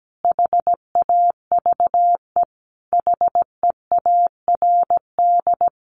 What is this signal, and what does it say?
Morse code, heard as a keyed tone, about 17 wpm, 700 Hz. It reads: HAVE HEARD
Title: Sentences from Top 1300 Words 17wpm